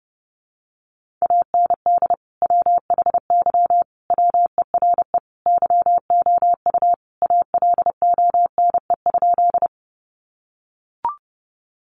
Morse code, heard as a keyed tone, about 30 wpm, 700 Hz. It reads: AND WHY WERE YOU ALONE?
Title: Sentences from Top 1000 Words 30wpm